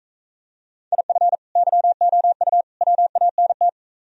Morse code, heard as a keyed tone, about 45 wpm, 700 Hz. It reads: IF YOU WANT